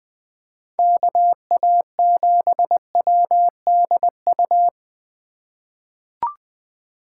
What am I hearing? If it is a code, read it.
Morse code, heard as a keyed tone, about 20 wpm, 700 Hz. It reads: KA7WDU
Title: USA Call Signs - 6 Characters - 3x Character Spacing - 20wpm